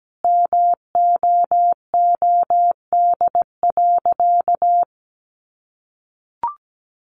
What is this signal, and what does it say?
Morse code, heard as a keyed tone, about 17 wpm, 700 Hz. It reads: MOOD.
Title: Sentences from Top 1100 Words 17wpm